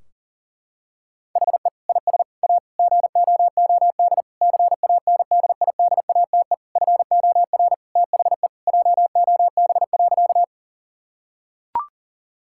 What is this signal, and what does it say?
Morse code, heard as a keyed tone, about 40 wpm, 700 Hz. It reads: HE IS A GOOD CANDIDATE FOR THE JOB.
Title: Sentences from Top 1600 Words 40wpm